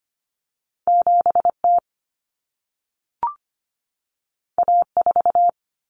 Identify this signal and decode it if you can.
Morse code, heard as a keyed tone, about 25 wpm, 700 Hz. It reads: 7T A4
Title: Common Cut Numbers - 2-Digit 25wpm